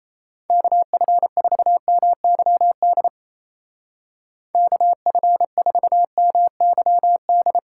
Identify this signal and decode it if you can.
Morse code, heard as a keyed tone, about 37 wpm, 700 Hz. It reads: KF4MYD KF4MYD